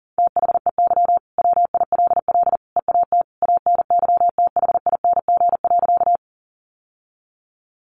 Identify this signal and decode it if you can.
Morse code, heard as a keyed tone, about 40 wpm, 700 Hz. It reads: THEY WILL EAT ANYTHING.
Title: Sentences from Top 1000 Words 40wpm